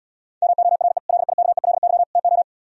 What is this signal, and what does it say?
Morse code, heard as a keyed tone, about 77 wpm, 700 Hz. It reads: MORE LIKELY TO